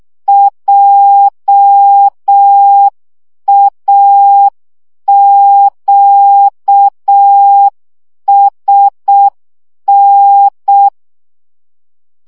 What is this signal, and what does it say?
Morse code, heard as a keyed tone, about 6 wpm, 800 Hz. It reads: JAQSN
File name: morze vizsga 2011-3 5